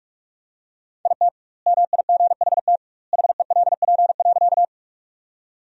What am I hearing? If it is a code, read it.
Morse code, heard as a keyed tone, about 46 wpm, 700 Hz. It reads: IT MIGHT HELP.
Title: 2x Word-Spacing - Sentences from Top 300 Words - 45wpm